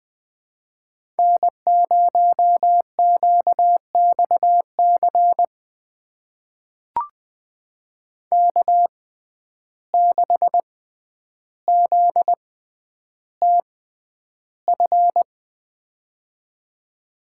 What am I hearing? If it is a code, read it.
Morse code, heard as a keyed tone, about 20 wpm, spaced as for 10 wpm, 700 Hz. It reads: N0QXC K6ZTF
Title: USA Call Signs - 5 Characters - 6x Character Spacing - 20wpm